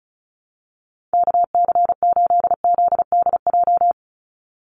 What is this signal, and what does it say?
Morse code, heard as a keyed tone, about 34 wpm, 700 Hz. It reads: KC8ZDJ
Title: USA Call Signs 35wpm